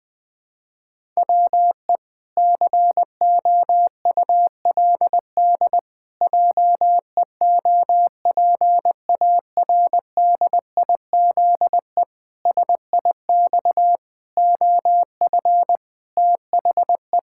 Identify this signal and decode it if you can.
Morse code, heard as a keyed tone, about 20 wpm, 700 Hz. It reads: WE COULD JEOPARDIZE SIX OF THE
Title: Pangram Challenge v2 - 20wpm